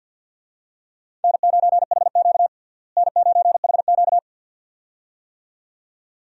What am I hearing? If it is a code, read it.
Morse code, heard as a keyed tone, about 46 wpm, 700 Hz. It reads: N9HQ N9HQ
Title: USA Call Signs - 4 Characters - Repeated 2x - 50wpm